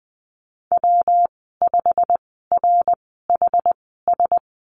Morse code, heard as a keyed tone, about 20 wpm, spaced as for 14 wpm, 700 Hz. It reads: W5RHS
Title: 186 Lesson 30g - Letter b - Call Signs 20wpm